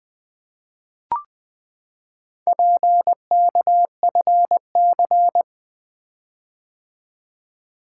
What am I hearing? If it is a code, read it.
Morse code, heard as a keyed tone, about 20 wpm, 700 Hz. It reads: PKFC
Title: Four Letter-Number Combinations - 20wpm